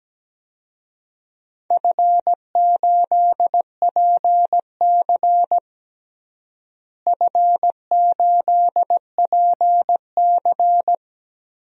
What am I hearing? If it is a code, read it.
Morse code, heard as a keyed tone, about 17 wpm, 700 Hz. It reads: F8PC F8PC
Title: DXCC-100 Call Signs - 4 Characters - Repeated 3x - 17wpm